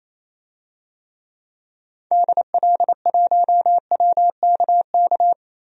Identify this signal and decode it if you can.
Morse code, heard as a keyed tone, about 28 wpm, 700 Hz. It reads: DL1WKK
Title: DXCC-100 Call Signs - 6 Characters - 28wpm